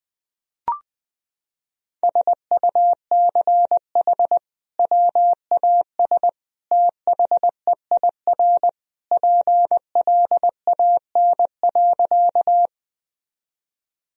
Morse code, heard as a keyed tone, about 21 wpm, 700 Hz. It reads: SUCH WAS THEIR PLAN.
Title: Sentences from Top 1000 Words - Review Part 2 - 20wpm